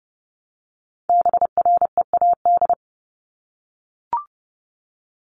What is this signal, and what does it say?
Morse code, heard as a keyed tone, about 30 wpm, 700 Hz. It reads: BREAD
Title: Top 1501-1750 Words 30wpm